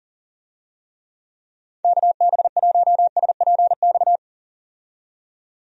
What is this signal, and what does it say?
Morse code, heard as a keyed tone, about 38 wpm, 700 Hz. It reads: KB1SPX
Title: USA Call Signs - 6 Characters - 40wpm